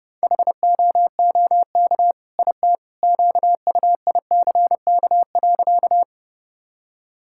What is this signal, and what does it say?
Morse code, heard as a keyed tone, about 32 wpm, 700 Hz. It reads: HOOK IT QUICK.
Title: Sentences from Top 1600 Words 30wpm